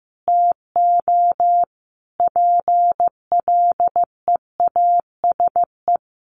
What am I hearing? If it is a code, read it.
Morse code, heard as a keyed tone, about 15 wpm, 700 Hz. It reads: TO PLEASE
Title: Sentences from Top 1500 Words 15wpm